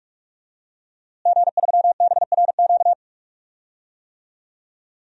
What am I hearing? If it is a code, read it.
Morse code, heard as a keyed tone, about 42 wpm, 700 Hz. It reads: G3BRQ
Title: DXCC-100 Call Signs - 5 Characters - Repeated 2x - 45wpm